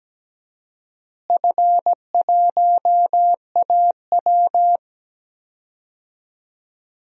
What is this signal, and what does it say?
Morse code, heard as a keyed tone, about 17 wpm, 700 Hz. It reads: F1AW
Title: DXCC-100 Call Signs - 4 Characters - Repeated 2x - 17wpm